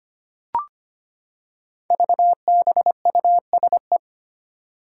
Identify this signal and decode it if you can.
Morse code, heard as a keyed tone, about 25 wpm, 700 Hz. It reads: VBUSE
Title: Five Challenging Letter-Number Combinations 25wpm